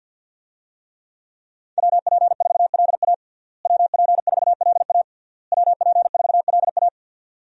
Morse code, heard as a keyed tone, about 52 wpm, 700 Hz. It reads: WP4LA WP4LA WP4LA
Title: USA Call Signs - 5 Characters - Repeated 3x - 50wpm